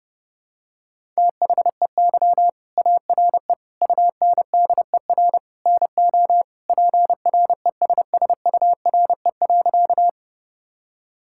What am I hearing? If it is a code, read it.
Morse code, heard as a keyed tone, about 30 wpm, 700 Hz. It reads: THEY ARE UNDER NO PRESSURE.
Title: Sentences from Top 1000 Words - Review Part 6 - 30wpm